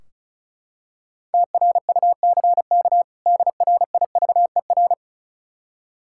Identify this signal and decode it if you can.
Morse code, heard as a keyed tone, about 36 wpm, 700 Hz. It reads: TRUCK DRIVER
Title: K1USN SST QSO Practice 35wpm